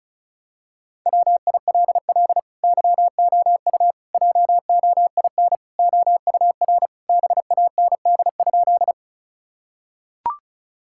Morse code, heard as a keyed tone, about 34 wpm, 700 Hz. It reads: WILL YOU JOIN OUR BAND?
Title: Sentences from Top 1600 Words 35wpm